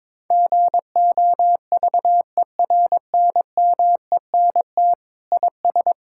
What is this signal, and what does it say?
Morse code, heard as a keyed tone, about 22 wpm, 700 Hz. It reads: GOVERNMENT IS
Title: Sentences from Top 1000 Words - Review Part 6 - 22wpm